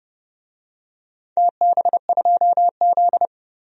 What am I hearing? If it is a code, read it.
Morse code, heard as a keyed tone, about 30 wpm, 700 Hz. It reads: TB2Z
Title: DXCC-100 Call Signs - 4 Characters - 30wpm